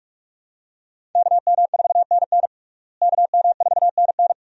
Morse code, heard as a keyed tone, about 47 wpm, 700 Hz. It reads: KM4NN KM4NN
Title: USA Call Signs - 5 Characters - Repeated 2x - 45wpm